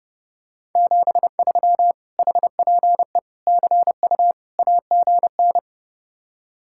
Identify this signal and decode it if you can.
Morse code, heard as a keyed tone, about 30 wpm, 700 Hz. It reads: 73 HPE CU AGN